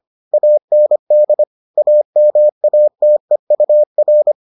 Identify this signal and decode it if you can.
Morse code, heard as a keyed tone, about 25 wpm, 600 Hz. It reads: AND AMATEUR